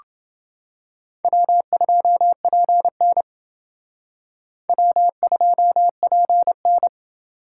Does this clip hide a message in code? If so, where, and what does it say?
Yes: Morse code from 1.2 to 6.9 s, heard as a keyed tone, about 29 wpm, 700 Hz. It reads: W2PN W2PN